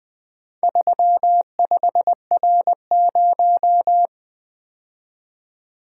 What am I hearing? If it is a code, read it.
Morse code, heard as a keyed tone, about 20 wpm, 700 Hz. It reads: 35R0
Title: Four Letter-Number Combinations - 20wpm